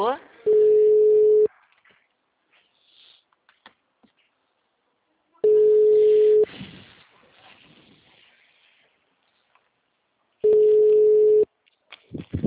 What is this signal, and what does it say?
Ringback tone (Europe, 425 Hz), with speech.